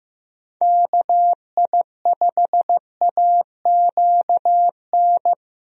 Morse code, heard as a keyed tone, about 15 wpm, 700 Hz. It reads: KI5AQN
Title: USA Call Signs - 6 Characters - Repeated 2x - 15wpm